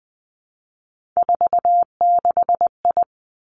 Morse code, heard as a keyed tone, about 20 wpm, 700 Hz. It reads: 46I
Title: Three Challenging Letter-Number Combinations 20wpm